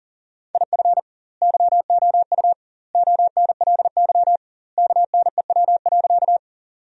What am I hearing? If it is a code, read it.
Morse code, heard as a keyed tone, about 40 wpm, 700 Hz. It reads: IF YOU ONLY KNEW.
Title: 2x Word-Spacing - Sentences from Top 400 Words - 40wpm